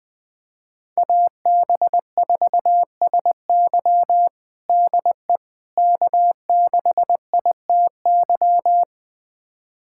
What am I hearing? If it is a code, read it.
Morse code, heard as a keyed tone, about 20 wpm, 700 Hz. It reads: AB4SY DE K6ITY
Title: USA Callsign DE Callsign - 5 Characters - 20wpm